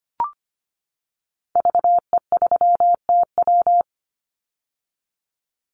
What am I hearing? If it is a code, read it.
Morse code, heard as a keyed tone, about 25 wpm, 700 Hz. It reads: VE3TW
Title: Call Signs - 25wpm